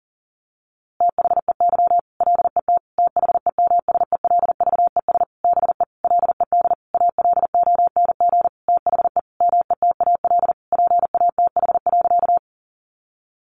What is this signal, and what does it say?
Morse code, heard as a keyed tone, about 40 wpm, 700 Hz. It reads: THEY LET THEMSELVES BE LED ALONG THE METAL PATH.
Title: A Sound Of Thunder 40wpm